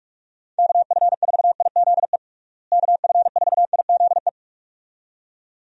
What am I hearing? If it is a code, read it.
Morse code, heard as a keyed tone, about 42 wpm, 700 Hz. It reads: KF4IZE KF4IZE
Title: USA Call Signs - 6 Characters - Repeated 2x - 45wpm